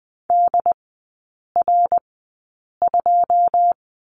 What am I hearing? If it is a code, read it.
Morse code, heard as a keyed tone, about 20 wpm, spaced as for 8 wpm, 700 Hz. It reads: DR2
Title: 121 Lesson 20e - Number 3 - Three Characters 20wpm